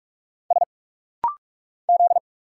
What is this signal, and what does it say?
Morse code, heard as a keyed tone, about 44 wpm, 700 Hz. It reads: S Z
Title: Single Letter-Number Alt - Rapid-Fire 45wpm